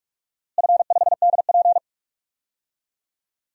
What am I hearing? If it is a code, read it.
Morse code, heard as a keyed tone, about 44 wpm, 700 Hz. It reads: F5DP
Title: DXCC-100 Call Signs - 4 Characters - Repeated 2x - 45wpm